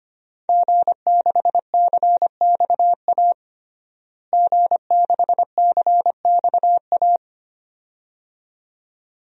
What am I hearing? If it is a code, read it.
Morse code, heard as a keyed tone, about 25 wpm, 700 Hz. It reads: G6CXA G6CXA
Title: DXCC-100 Call Signs - 5 Characters - Repeated 2x - 25wpm